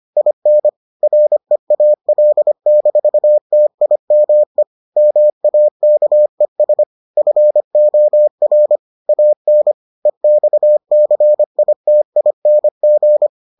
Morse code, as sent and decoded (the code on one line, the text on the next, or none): .. -. / .-. .- .-.. -....- - .. -- . / -- .- -.- . ... / ..-. --- .-. / .- -. / . -..- -.-. .. - .. -. --.
IN REAL-TIME MAKES FOR AN EXCITING